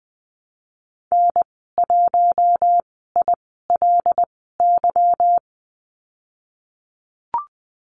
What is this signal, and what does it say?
Morse code, heard as a keyed tone, about 20 wpm, spaced as for 15 wpm, 700 Hz. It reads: N1ILY
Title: 167 Lesson 27g - Number 7 - Call Signs 20wpm